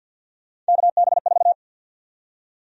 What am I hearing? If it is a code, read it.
Morse code, heard as a keyed tone, about 52 wpm, 700 Hz. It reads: KB4
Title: USA Call Sign Prefixes - 50wpm